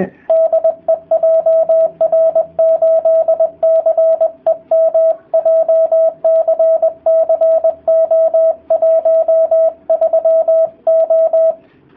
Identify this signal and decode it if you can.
Morse code, heard as a keyed tone, about 20 wpm, 650 Hz. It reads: DE JR8CEM JCC O13O